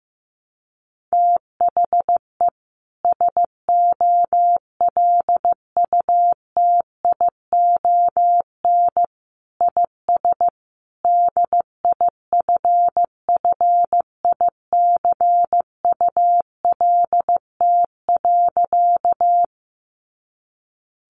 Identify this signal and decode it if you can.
Morse code, heard as a keyed tone, about 15 wpm, 700 Hz. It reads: THE SOLUTION IS DIFFICULT.
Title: Sentences from Top 1300 Words 15wpm